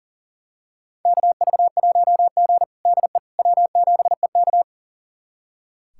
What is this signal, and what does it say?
Morse code, heard as a keyed tone, about 40 wpm, 700 Hz. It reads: KV1G DE W7EK
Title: USA Callsign DE Callsign - 4 Characters - 40wpm